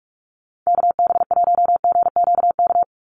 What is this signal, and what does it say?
Morse code, heard as a keyed tone, about 46 wpm, 690 Hz. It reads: KB1GQK
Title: USA Call Signs 45wpm